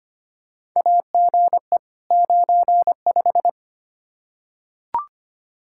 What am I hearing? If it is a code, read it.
Morse code, heard as a keyed tone, about 25 wpm, 700 Hz. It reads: AGE 95
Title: K1USN SST QSO Practice 25wpm